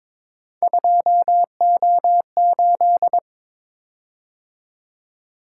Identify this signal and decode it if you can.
Morse code, heard as a keyed tone, about 22 wpm, 700 Hz. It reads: 2O8
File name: practice / 22wpm / Three Letter-Number Combinations - 22wpm